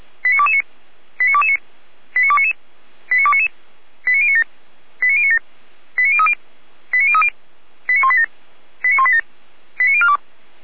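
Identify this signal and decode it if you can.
Five-tone selective call (ZVEI), heard as quick runs of five tones, tones of about 70 ms. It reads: 89229 89229 89299 89299 89097 89097 89030 89030 89187 89187 89042
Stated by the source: ZVEI five-tone call received by SDR, from the feuerwehr-orlamuende.org fire brigade audio server.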